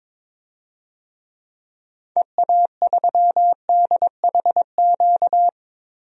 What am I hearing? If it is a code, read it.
Morse code, heard as a keyed tone, about 21 wpm, 700 Hz. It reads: EA3DHQ
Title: DXCC-100 Call Signs - 6 Characters - 22wpm